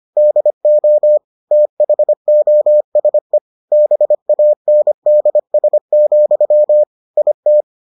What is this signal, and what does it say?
Morse code, heard as a keyed tone, about 25 wpm, 600 Hz. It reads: DO THOSE BANDS, IT